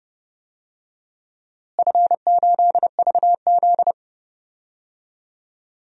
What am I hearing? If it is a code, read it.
Morse code, heard as a keyed tone, about 30 wpm, 700 Hz. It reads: F8VZ
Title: DXCC-100 Call Signs - 4 Characters - Repeated 2x - 30wpm